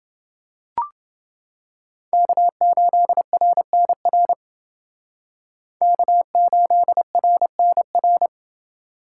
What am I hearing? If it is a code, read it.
Morse code, heard as a keyed tone, about 30 wpm, 700 Hz. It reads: K8RNR K8RNR